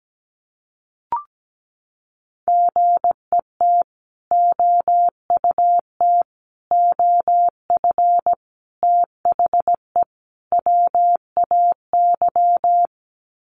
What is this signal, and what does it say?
Morse code, heard as a keyed tone, about 17 wpm, 700 Hz. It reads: GET OUT OF THE WAY
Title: Sets of 5 Words Encore 17wpm